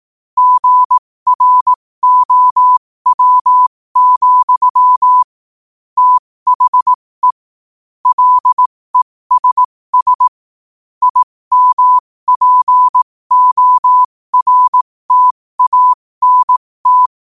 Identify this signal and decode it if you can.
Morse code, heard as a keyed tone, about 18 wpm, spaced as for 15 wpm, 1000 Hz. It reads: GROW, THE LESS IMPORTANT
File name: qotdfiles / 15wpm